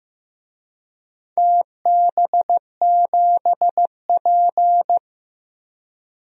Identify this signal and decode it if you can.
Morse code, heard as a keyed tone, about 15 wpm, 700 Hz. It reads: TB7P
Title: DXCC-100 Call Signs - 4 Characters - Repeated 3x - 15wpm